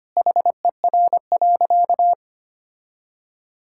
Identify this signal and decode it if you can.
Morse code, heard as a keyed tone, about 25 wpm, 700 Hz. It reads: HER.
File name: practice / 25wpm / Sentences with 2-Word Phrases - 25wpm